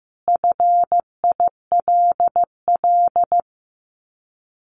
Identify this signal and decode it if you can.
Morse code, heard as a keyed tone, about 15 wpm, 700 Hz. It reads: FILL
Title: Top 500 Words Repeated 15wpm